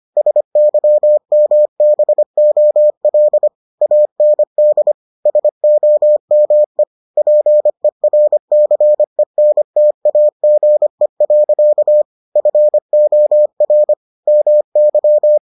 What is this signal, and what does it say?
Morse code, heard as a keyed tone, about 25 wpm, 600 Hz. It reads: SYMBOL AND SOME PERCENTAGE. FOR MY